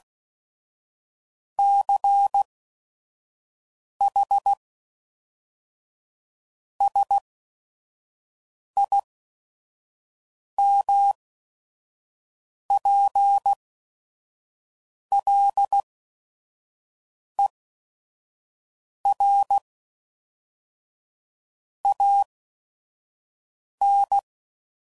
Morse code, heard as a keyed tone, about 16 wpm, spaced as for 5 wpm, 800 Hz. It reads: CH SIMPLER AN
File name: CW Code Groups @5WPM / Technical Subjects / AND Gate